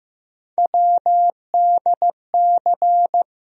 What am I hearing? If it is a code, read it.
Morse code, heard as a keyed tone, about 15 wpm, 700 Hz. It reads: WDC
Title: USA Call Sign Postfixes - 3 Letters 15wpm